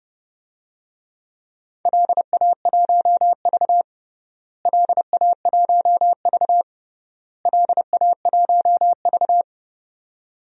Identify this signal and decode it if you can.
Morse code, heard as a keyed tone, about 30 wpm, 700 Hz. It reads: LA1V LA1V LA1V
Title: DXCC-100 Call Signs - 4 Characters - Repeated 3x - 30wpm